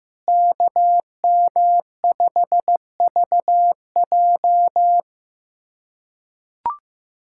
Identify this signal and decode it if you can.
Morse code, heard as a keyed tone, about 15 wpm, 700 Hz. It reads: KM5VJ
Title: USA Callsign DE Callsign - 5 Characters - 15wpm